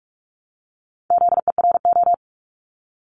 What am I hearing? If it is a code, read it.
Morse code, heard as a keyed tone, about 46 wpm, 700 Hz. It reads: ZERO
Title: Multi-letter Phonemes Repeated 3x 45wpm